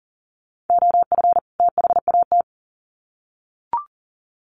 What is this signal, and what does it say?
Morse code, heard as a keyed tone, about 40 wpm, 700 Hz. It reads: OF THAT
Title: Sets of 2 Words 40wpm